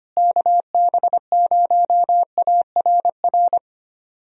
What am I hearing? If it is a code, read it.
Morse code, heard as a keyed tone, about 25 wpm, 700 Hz. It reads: KB0ARR
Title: USA Call Signs - 6 Characters - Repeated 3x - 25wpm